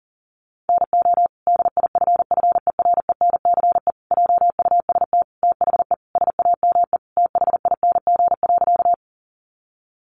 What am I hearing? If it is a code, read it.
Morse code, heard as a keyed tone, about 40 wpm, 700 Hz. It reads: NO DIFFERENCE JUST THE SAME THING.
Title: Sentences from Top 800 Words 40wpm